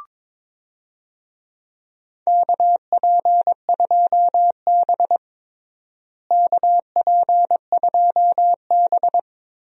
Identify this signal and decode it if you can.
Morse code, heard as a keyed tone, about 21 wpm, 700 Hz. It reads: KP2B KP2B
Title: DXCC-100 Call Signs - 4 Characters - Repeated 3x - 22wpm